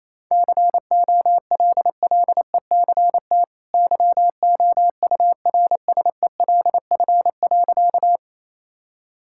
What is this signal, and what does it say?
Morse code, heard as a keyed tone, about 28 wpm, 700 Hz. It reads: COLLECT YOURSELF.
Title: Sentences from Top 1600 Words 28wpm